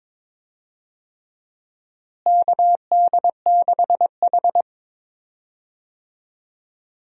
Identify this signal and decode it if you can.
Morse code, heard as a keyed tone, about 22 wpm, 700 Hz. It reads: KD6H
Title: USA Call Signs - 4 Characters - Repeated 3x - 22wpm